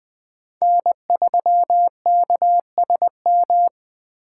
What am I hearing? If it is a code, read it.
Morse code, heard as a keyed tone, about 20 wpm, 700 Hz. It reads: N3KSM
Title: Speed Racing - USA 1x3 Call Signs 17wpm